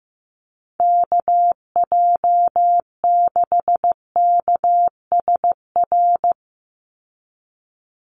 Morse code, heard as a keyed tone, about 15 wpm, 700 Hz. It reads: KJ6KSR
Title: USA Call Signs 15wpm